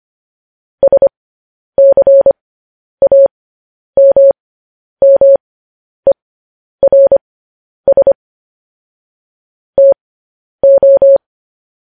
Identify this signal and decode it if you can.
Morse code, heard as a keyed tone, about 25 wpm, spaced as for 10 wpm, 550 Hz. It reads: SCAMMERS TO
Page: news headlines in morse code at 10 wpm